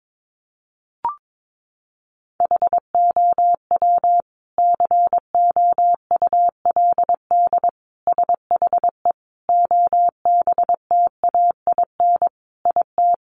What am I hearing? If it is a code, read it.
Morse code, heard as a keyed tone, about 22 wpm, 700 Hz. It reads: HOW COULD SHE OBTAIN IT
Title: Sentences from Top 1300 Words 22wpm